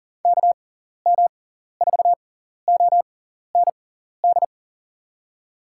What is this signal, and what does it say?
Morse code, heard as a keyed tone, about 40 wpm, spaced as for 14 wpm, 700 Hz. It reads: KM4OND
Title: USA Call Signs - 6 Characters - 6x Character Spacing - 40wpm